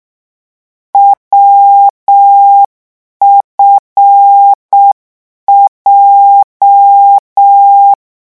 Morse code, heard as a keyed tone, about 6 wpm, 800 Hz. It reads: WFJ